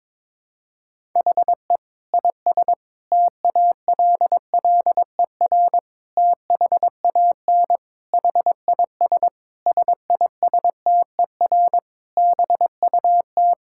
Morse code, heard as a keyed tone, about 22 wpm, 700 Hz. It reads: HE IS TALLER THAN HIS SISTER BUT
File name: Difficult Sentences with Hints - 22wpm